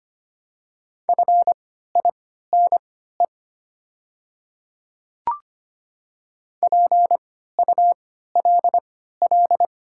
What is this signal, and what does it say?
Morse code, heard as a keyed tone, about 25 wpm, spaced as for 14 wpm, 700 Hz. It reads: FINE PULL
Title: 3x Character Spacing - Top 500 Words - 25wpm